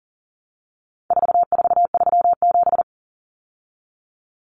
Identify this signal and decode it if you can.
Morse code, heard as a keyed tone, about 40 wpm, 700 Hz. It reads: 4437
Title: Numbers - 4-Digit 40wpm